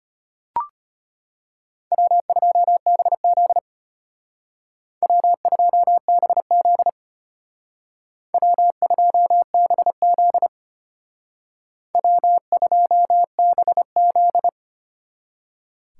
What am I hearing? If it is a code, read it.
Morse code, heard as a keyed tone, about 32 wpm, 700 Hz. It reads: W2BZ W2BZ W2BZ W2BZ